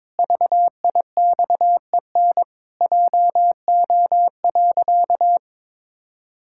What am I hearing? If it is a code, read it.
Morse code, heard as a keyed tone, about 22 wpm, 700 Hz. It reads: VIXEN JO.
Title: Pangram Challenge v2 - 22wpm